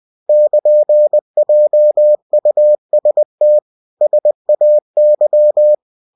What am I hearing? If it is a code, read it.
Morse code, heard as a keyed tone, about 20 wpm, 600 Hz. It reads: <KN>JUST SAY
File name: cw-books / winnie-the-pooh / winnie-20wpm-page-0104